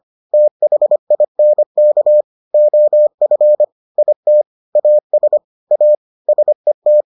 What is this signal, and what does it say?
Morse code, heard as a keyed tone, about 25 wpm, 600 Hz. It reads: THINK OF IT AS A SET